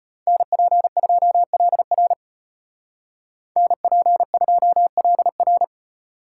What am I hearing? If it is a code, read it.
Morse code, heard as a keyed tone, about 33 wpm, 700 Hz. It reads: NP2LR NP2LR